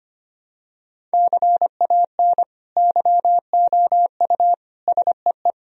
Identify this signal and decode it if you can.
Morse code, heard as a keyed tone, about 25 wpm, 700 Hz. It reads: CAN YOU SEE
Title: Sentences from Top 1000 Words - Review Part 1 - 25wpm